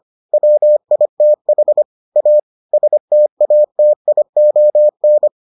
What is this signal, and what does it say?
Morse code, heard as a keyed tone, about 25 wpm, 600 Hz. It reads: WITH A STATION